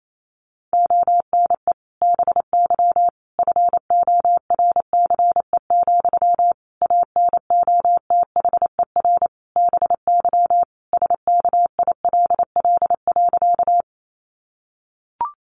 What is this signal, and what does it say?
Morse code, heard as a keyed tone, about 28 wpm, 700 Hz. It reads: ONE BY FORCE, ANOTHER BY SKILL.
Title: Sentences from Top 1200 Words 28wpm